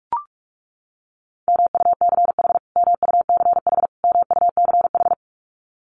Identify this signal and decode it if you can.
Morse code, heard as a keyed tone, about 44 wpm, 700 Hz. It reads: MUCH MUCH MUCH
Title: Top 300 Words Repeated 45wpm